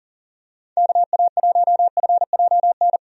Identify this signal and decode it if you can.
Morse code, heard as a keyed tone, about 40 wpm, 700 Hz. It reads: KA1FJN
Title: USA Call Signs - 6 Characters - 40wpm